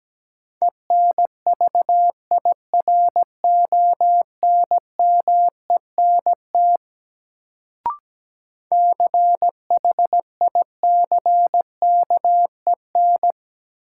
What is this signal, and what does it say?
Morse code, heard as a keyed tone, about 17 wpm, 700 Hz. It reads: ENVIRONMENT CHICKEN
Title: Sentences from Top 1000 Words - Review Part 5 - New Word Drill - 17wpm